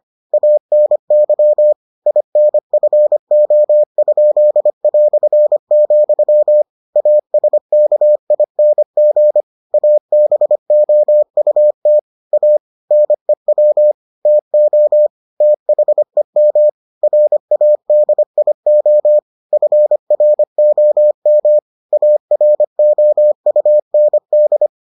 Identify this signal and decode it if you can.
Morse code, heard as a keyed tone, about 25 wpm, 600 Hz. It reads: ANY INFO?", ASKING ABOUT A NEW TO THEM RADIO FROM AROUND